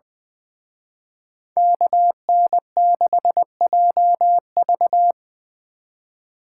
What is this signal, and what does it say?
Morse code, heard as a keyed tone, about 23 wpm, 700 Hz. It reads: KN6JV
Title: Speed Racing - USA Call Signs 20wpm